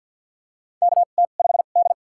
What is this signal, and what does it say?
Morse code, heard as a keyed tone, about 48 wpm, spaced as for 35 wpm, 700 Hz. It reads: KT5D
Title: USA Call Signs - 4 Characters - 2x Character Spacing - 50wpm